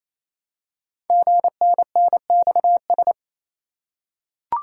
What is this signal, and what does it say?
Morse code, heard as a keyed tone, about 27 wpm, 700 Hz. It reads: GNNXS
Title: Code Groups - 5 Letters - 28wpm